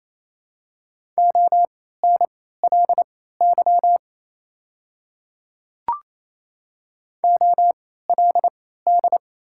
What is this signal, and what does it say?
Morse code, heard as a keyed tone, about 28 wpm, spaced as for 16 wpm, 690 Hz. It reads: ONLY OLD
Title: 3x Character Spacing - Top 200 Words - 28wpm